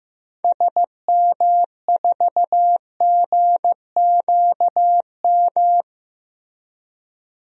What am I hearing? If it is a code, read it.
Morse code, heard as a keyed tone, about 15 wpm, 700 Hz. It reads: SM4GQM
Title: DXCC-100 Call Signs - 6 Characters - 15wpm